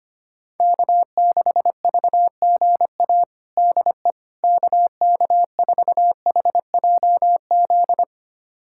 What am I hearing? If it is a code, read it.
Morse code, heard as a keyed tone, about 25 wpm, 700 Hz. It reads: K6VGA DE KK4HJZ
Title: USA Callsign DE Callsign 25wpm